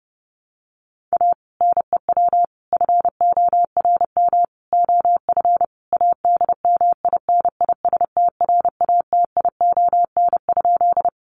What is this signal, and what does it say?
Morse code, heard as a keyed tone, about 30 wpm, 700 Hz. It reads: A NEW FORM OF ADMINISTRATION?
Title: Sentences from Top 1300 Words 30wpm